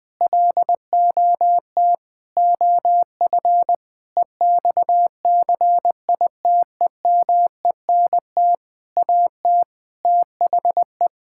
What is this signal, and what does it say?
Morse code, heard as a keyed tone, about 20 wpm, 700 Hz. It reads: LOT OF EXCITEMENT AT THE
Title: Sentences with 7-Word Phrases - Spoken Beginning - 20wpm